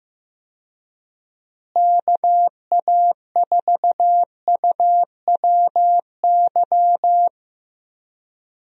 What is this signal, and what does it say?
Morse code, heard as a keyed tone, about 15 wpm, 700 Hz. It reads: KA4UWY
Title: USA Call Signs - 6 Characters - Repeated 3x - 15wpm